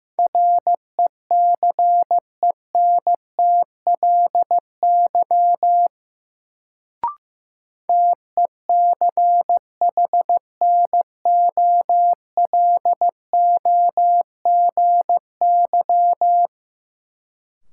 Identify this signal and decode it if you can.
Morse code, heard as a keyed tone, about 15 wpm, 700 Hz. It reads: RECENTLY TECHNOLOGY
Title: Sentences from Top 1200 Words - New Word Drill Alt - 15wpm